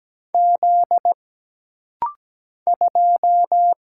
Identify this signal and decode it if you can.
Morse code, heard as a keyed tone, about 17 wpm, 700 Hz. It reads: Z 2
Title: Single Letter-Number - Rapid-Fire 17wpm